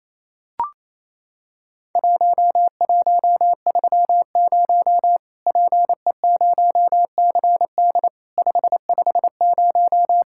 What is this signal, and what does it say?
Morse code, heard as a keyed tone, about 27 wpm, 700 Hz. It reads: 1130 PE0CD 550
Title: Worked All Europe DX Contest - QTC Exchange - 28wpm